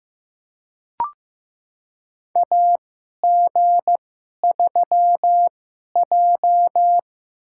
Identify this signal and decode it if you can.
Morse code, heard as a keyed tone, about 15 wpm, spaced as for 11 wpm, 700 Hz. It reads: AG3J
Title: USA Call Signs - 4 Characters - 2x Character Spacing - 15wpm